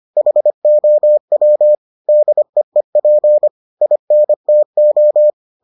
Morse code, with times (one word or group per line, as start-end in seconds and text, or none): 0.2-1.8 HOW
2.1-3.5 DEEP
3.8-5.3 INTO